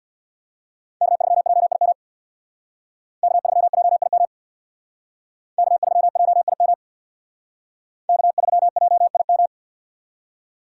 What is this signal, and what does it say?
Morse code, heard as a keyed tone, about 59 wpm, 700 Hz. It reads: K3JIM K3JIM K3JIM K3JIM
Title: Speed Racing - USA 1x3 Call Signs 50wpm